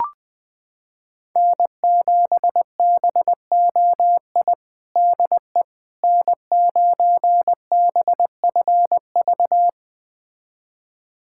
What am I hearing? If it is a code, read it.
Morse code, heard as a keyed tone, about 20 wpm, 700 Hz. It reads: N7BOI DE N9BFV
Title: USA Callsign DE Callsign - 5 Characters - 20wpm